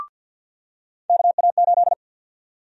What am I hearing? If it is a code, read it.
Morse code, heard as a keyed tone, about 53 wpm, 700 Hz. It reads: KA8
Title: USA Call Sign Prefixes - 50wpm